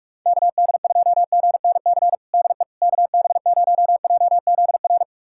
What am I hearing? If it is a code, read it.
Morse code, heard as a keyed tone, about 42 wpm, 700 Hz. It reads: KD2GNC DE KB0JZR
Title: USA Callsign DE Callsign - 6 Characters - 45wpm